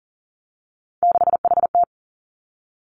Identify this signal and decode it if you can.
Morse code, heard as a keyed tone, about 41 wpm, 700 Hz. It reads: 6HT
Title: Three Challenging Letter-Number Combinations 40wpm